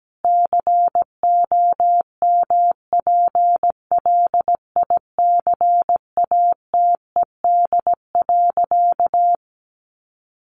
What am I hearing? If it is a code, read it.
Morse code, heard as a keyed tone, about 17 wpm, 700 Hz. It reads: COMPLICATED.
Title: Sentences from Top 1200 Words 17wpm